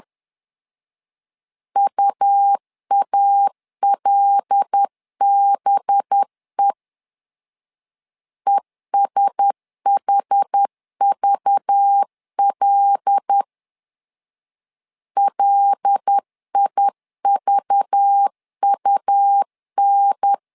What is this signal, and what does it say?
Morse code, heard as a keyed tone, about 10 wpm, 800 Hz. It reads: UALBE ESHVL LIVUN
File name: LEZIONI / 5 LBVC / 5 LBV 10wpm 2248